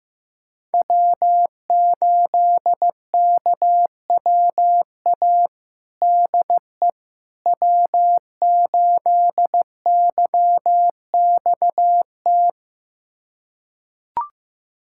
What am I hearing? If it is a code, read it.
Morse code, heard as a keyed tone, about 15 wpm, 700 Hz. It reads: W8KWA DE W8YXT
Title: USA Callsign DE Callsign - 5 Characters - 15wpm